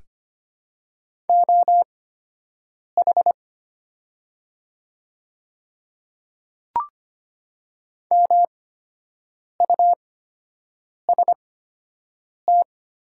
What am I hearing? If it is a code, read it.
Morse code, heard as a keyed tone, about 25 wpm, spaced as for 7 wpm, 700 Hz. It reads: OH MUST
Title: 8x Character Spacing - Top 500 Words - 25wpm